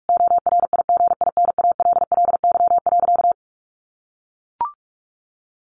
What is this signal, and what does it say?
Morse code, heard as a keyed tone, about 44 wpm, 700 Hz. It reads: ORIGINALLY.
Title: Sentences from Top 1500 Words 45wpm